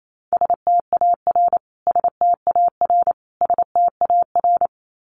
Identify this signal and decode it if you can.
Morse code, heard as a keyed tone, about 28 wpm, 700 Hz. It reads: STAR STAR STAR
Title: Top 500 Words Repeated 28wpm